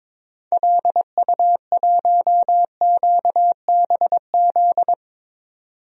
Morse code, heard as a keyed tone, about 22 wpm, 700 Hz. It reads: LU1QBZ
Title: DXCC-100 Call Signs - 6 Characters - 22wpm